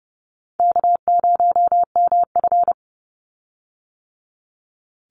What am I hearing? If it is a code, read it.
Morse code, heard as a keyed tone, about 30 wpm, 700 Hz. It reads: K0MF
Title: USA Call Signs 30wpm